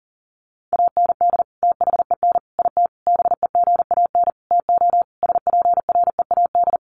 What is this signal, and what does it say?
Morse code, heard as a keyed tone, about 40 wpm, 700 Hz. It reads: AND THEN IT BEGAN TO SPREAD
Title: Sentences from Top 1200 Words 40wpm